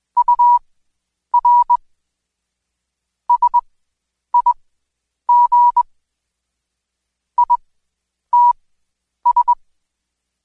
Morse code, heard as a keyed tone, about 21 wpm, spaced as for 9 wpm, 970 Hz. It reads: UR SIG ITS
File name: Code Quick / AudioDownld2 / Seg116-Day21-DayDrill 21